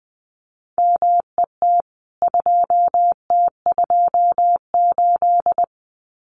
Morse code, heard as a keyed tone, about 20 wpm, 700 Hz. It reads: MET 2T28
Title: CWT Exchange 20wpm